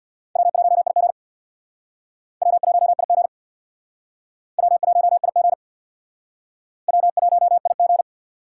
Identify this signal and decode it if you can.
Morse code, heard as a keyed tone, about 61 wpm, 700 Hz. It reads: W1IG W1IG W1IG W1IG
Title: Speed Racing - USA 1x2 Call Signs 50wpm